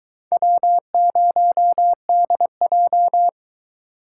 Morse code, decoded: W0DJ

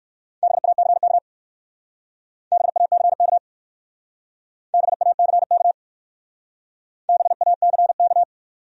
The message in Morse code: BACK BACK BACK BACK

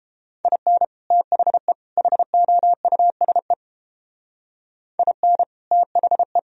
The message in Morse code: IN THE HOUSE IN THE

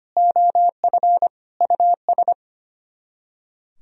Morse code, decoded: OF US

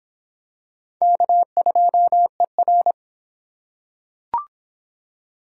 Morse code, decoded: K2ER

